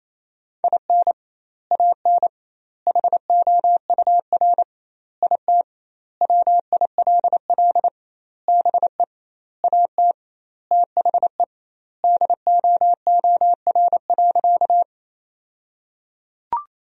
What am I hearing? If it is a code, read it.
Morse code, heard as a keyed tone, about 28 wpm, 700 Hz. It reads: IN AN HOUR IT WILL BE AT THE DOOR.
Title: 2x Word-Spacing - Sentences from Top 400 Words - 28wpm